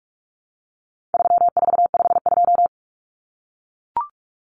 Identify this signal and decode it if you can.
Morse code, heard as a keyed tone, about 44 wpm, 700 Hz. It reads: 3452
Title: Numbers - 4-Digit 45wpm